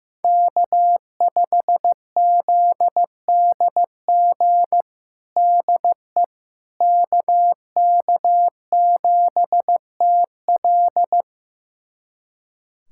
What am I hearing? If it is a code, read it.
Morse code, heard as a keyed tone, about 15 wpm, 700 Hz. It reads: K5ZDG DE KK7TL
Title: USA Callsign DE Callsign - 5 Characters - 15wpm